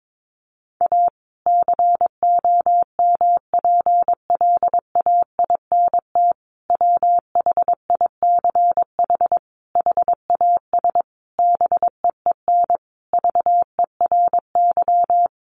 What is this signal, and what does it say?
Morse code, heard as a keyed tone, about 22 wpm, 700 Hz. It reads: A COMPLAINT WHICH HAS BEEN VERY
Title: Sentences from Top 1400 Words 22wpm